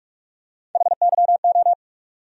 Morse code, heard as a keyed tone, about 43 wpm, 700 Hz. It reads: HYO
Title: USA Call Sign Postfixes - 3 Letters 45wpm